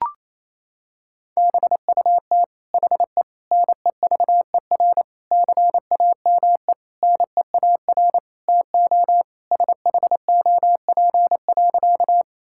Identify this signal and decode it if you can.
Morse code, heard as a keyed tone, about 28 wpm, 700 Hz. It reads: BUT HE NEVER CAME NEAR TO SHOP.
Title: Sentences from Top 1600 Words 28wpm